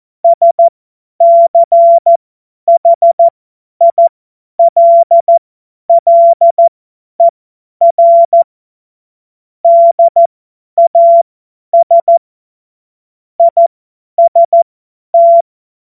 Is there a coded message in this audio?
Yes: Morse code from 0.2 to 15.4 s, heard as a keyed tone, about 14 wpm, spaced as for 10 wpm, 680 Hz. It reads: SCHILLER DAS IST